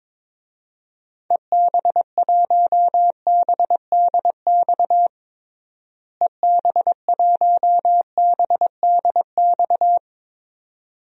Morse code, heard as a keyed tone, about 22 wpm, 700 Hz. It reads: EB1BDX EB1BDX
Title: DXCC-100 Call Signs - 6 Characters - Repeated 2x - 22wpm